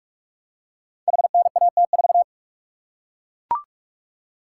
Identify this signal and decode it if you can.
Morse code, heard as a keyed tone, about 44 wpm, 700 Hz. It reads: HNAT4
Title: Code Groups - 5 Letter:Numbers - 45wpm